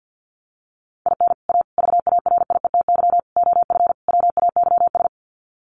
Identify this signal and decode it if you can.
Morse code, heard as a keyed tone, about 50 wpm, 700 Hz. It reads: IN A VARIETY OF WAYS
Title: Sets of 5 Words Encore 50wpm